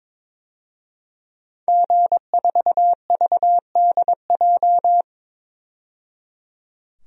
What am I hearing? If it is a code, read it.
Morse code, heard as a keyed tone, about 22 wpm, 700 Hz. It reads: G4VDJ
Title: DXCC-100 Call Signs - 5 Characters - 22wpm